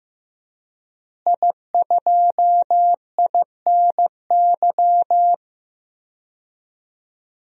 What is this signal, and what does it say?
Morse code, heard as a keyed tone, about 15 wpm, 700 Hz. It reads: I2INY
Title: DXCC-100 Call Signs - 5 Characters - Repeated 3x - 15wpm